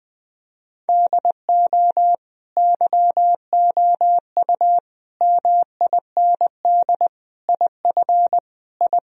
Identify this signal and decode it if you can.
Morse code, heard as a keyed tone, about 20 wpm, 700 Hz. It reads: DO YOU MIND IF I
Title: Sentences from Top 1000 Words - Review Part 2 - 20wpm